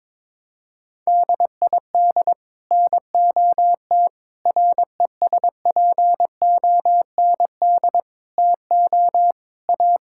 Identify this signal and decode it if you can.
Morse code, heard as a keyed tone, about 21 wpm, 700 Hz. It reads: DID NOT RESPOND TO A
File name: practice / 22wpm / Sets of 8 Words - 22wpm